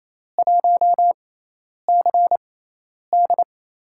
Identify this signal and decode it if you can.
Morse code, heard as a keyed tone, about 28 wpm, spaced as for 10 wpm, 700 Hz. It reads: 1CD
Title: USA Call Signs - 6x Character Spacing - 28wpm